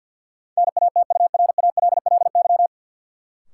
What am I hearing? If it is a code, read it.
Morse code, heard as a keyed tone, about 48 wpm, 700 Hz. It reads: NATURALLY
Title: Sentences from Top 1300 Words - New Word Drill - 50wpm